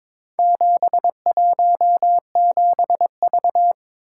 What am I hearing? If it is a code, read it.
Morse code, heard as a keyed tone, about 22 wpm, 700 Hz. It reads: 717V
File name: practice / 22wpm / Four Letter-Number Combinations - 22wpm